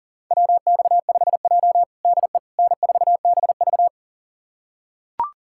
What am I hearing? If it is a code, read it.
Morse code, heard as a keyed tone, about 40 wpm, 700 Hz. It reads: WX5J DE N4BV
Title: USA Callsign DE Callsign - 4 Characters - 40wpm